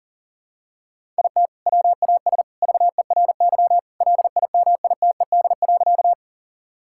Morse code, heard as a keyed tone, about 40 wpm, 700 Hz. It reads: IT WAS VERY LIMITED.
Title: Sentences from Top 1000 Words - Review Part 6 - 40wpm